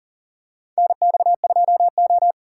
Morse code, heard as a keyed tone, about 40 wpm, 690 Hz. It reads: NX2O
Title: USA Call Signs - 4 Characters - 2x Character Spacing - 40wpm